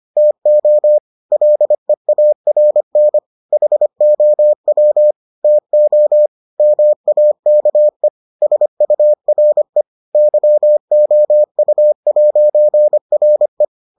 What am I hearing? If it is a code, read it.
Morse code, heard as a keyed tone, about 25 wpm, 600 Hz. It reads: TO LEARN HOW TO MAKE SURE YOU'RE